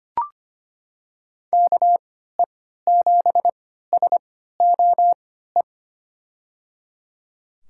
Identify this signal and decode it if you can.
Morse code, heard as a keyed tone, about 25 wpm, spaced as for 14 wpm, 700 Hz. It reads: KE7SOE
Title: USA Call Signs - 6 Characters - 3x Character Spacing - 25wpm